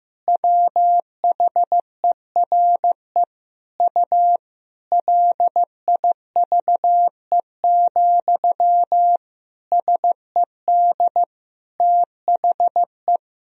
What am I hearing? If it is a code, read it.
Morse code, heard as a keyed tone, about 16 wpm, 700 Hz. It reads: WHERE U LIVE, SED THE
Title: Sentences from Top 200 Words and Abbreviations 15wpm